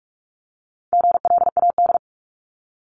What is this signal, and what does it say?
Morse code, heard as a keyed tone, about 46 wpm, 700 Hz. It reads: GLAD
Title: Multi-letter Phonemes 45wpm